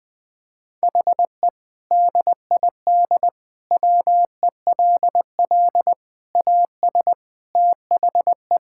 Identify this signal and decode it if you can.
Morse code, heard as a keyed tone, about 20 wpm, 700 Hz. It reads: HE DID WELL AS THE
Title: Sentences with 3-Word Phrases - 20wpm